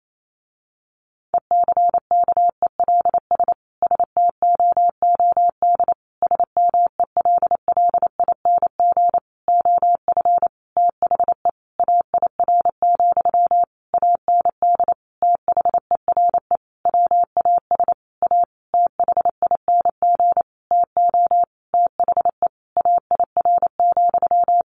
Morse code, heard as a keyed tone, about 28 wpm, 700 Hz. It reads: ECKELS STOOD SMELLING OF THE AIR, AND THERE WAS A THING TO THE AIR,
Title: A Sound Of Thunder 28wpm